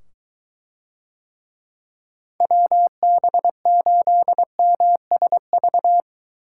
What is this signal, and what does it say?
Morse code, heard as a keyed tone, about 23 wpm, 700 Hz. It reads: WB8MSV